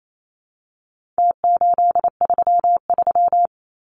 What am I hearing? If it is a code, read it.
Morse code, heard as a keyed tone, about 28 wpm, 700 Hz. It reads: T833
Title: Numbers - Contest Style - 4-Digit 28wpm